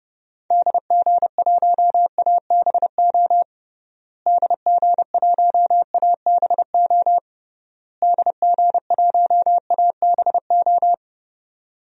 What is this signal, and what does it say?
Morse code, heard as a keyed tone, about 30 wpm, 700 Hz. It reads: DG1ABO DG1ABO DG1ABO
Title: DXCC-100 Call Signs - 6 Characters - Repeated 3x - 30wpm